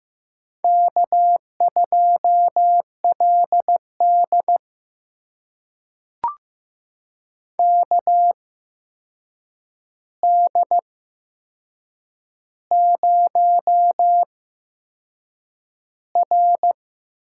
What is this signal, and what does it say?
Morse code, heard as a keyed tone, about 15 wpm, spaced as for 6 wpm, 700 Hz. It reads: K2LD KD0R
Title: USA Call Signs - 4 Characters - 8x Character Spacing - 15wpm